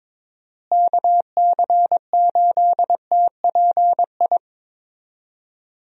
Morse code, heard as a keyed tone, about 21 wpm, 700 Hz. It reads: KC8TPI
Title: USA Call Signs - 6 Characters - Repeated 2x - 22wpm